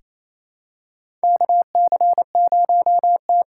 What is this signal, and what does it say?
Morse code, heard as a keyed tone, about 28 wpm, 700 Hz. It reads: KC0T
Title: USA Call Signs - 6 Characters - 6x Character Spacing - 28wpm